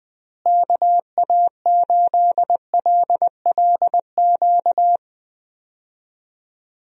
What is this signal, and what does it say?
Morse code, heard as a keyed tone, about 20 wpm, 700 Hz. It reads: KA8LLQ
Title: USA Call Signs - 6 Characters - Repeated 3x - 20wpm